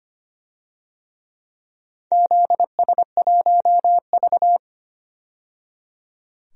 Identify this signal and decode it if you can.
Morse code, heard as a keyed tone, about 25 wpm, 700 Hz. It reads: ZS1V
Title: DXCC-100 Call Signs - 4 Characters - 25wpm